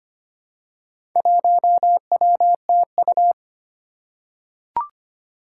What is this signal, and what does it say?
Morse code, heard as a keyed tone, about 25 wpm, 700 Hz. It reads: 1WTU
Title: Four Challenging Letter-Number Combinations 25wpm